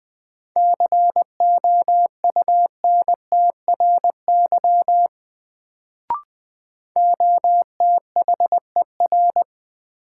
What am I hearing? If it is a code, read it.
Morse code, heard as a keyed tone, about 20 wpm, 700 Hz. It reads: COUNTRY OTHER
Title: Sets of 2 Words - New Word Drill - 20wpm